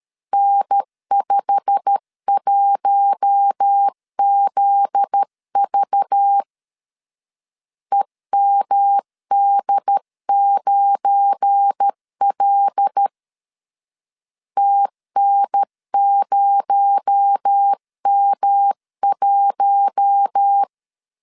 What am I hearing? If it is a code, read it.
Morse code, heard as a keyed tone, about 13 wpm, 800 Hz. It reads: N51ZV EMD9L TN0M1
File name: LEZIONI / 8 YJ45 / 8 YJ45 12wpm 2151